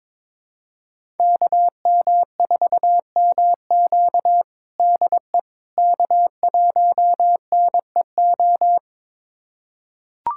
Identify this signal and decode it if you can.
Morse code, heard as a keyed tone, about 21 wpm, 700 Hz. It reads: KM4MQ DE K1NEO
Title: USA Callsign DE Callsign - 5 Characters - 22wpm